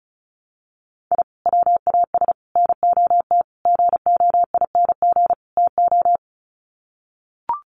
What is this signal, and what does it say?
Morse code, heard as a keyed tone, about 34 wpm, 700 Hz. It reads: I WAS NOT GOING TO
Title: Sets of 5 Words 35wpm